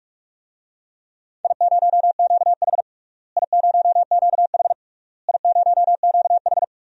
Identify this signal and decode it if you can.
Morse code, heard as a keyed tone, about 46 wpm, 700 Hz. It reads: I0QH I0QH I0QH
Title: DXCC-100 Call Signs - 4 Characters - Repeated 3x - 45wpm